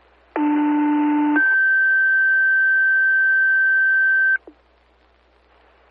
Two-tone page, heard as a one-second tone, then a three-second tone, about 300 then 1550 Hz. It is heard from 0.4 to 4.4 s.
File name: fire / rowe